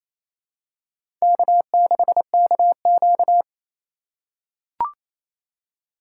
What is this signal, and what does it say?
Morse code, heard as a keyed tone, about 27 wpm, 700 Hz. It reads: K6KQ
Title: USA Call Signs - 4 Characters - 3x Character Spacing - 28wpm